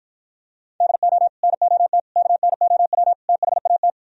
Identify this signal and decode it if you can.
Morse code, heard as a keyed tone, about 54 wpm, 700 Hz. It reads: DO NOT KNOW THAT